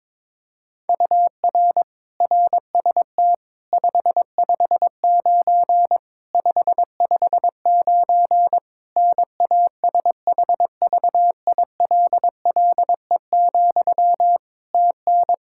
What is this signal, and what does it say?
Morse code, heard as a keyed tone, about 22 wpm, 700 Hz. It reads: UR RST 559 559 NASHVILLE, TN
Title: Practice QSO - Exchange 1 - Beginning - 22wpm